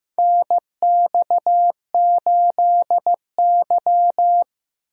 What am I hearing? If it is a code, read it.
Morse code, heard as a keyed tone, about 15 wpm, 700 Hz. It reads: NX8Y
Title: USA Call Signs - 4 Characters - 15wpm